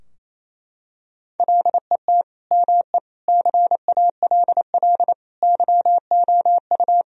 Morse code, heard as a keyed tone, about 27 wpm, 700 Hz. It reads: LET ME CALL YOU
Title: Sentences from Top 1000 Words - Review Part 2 - 28wpm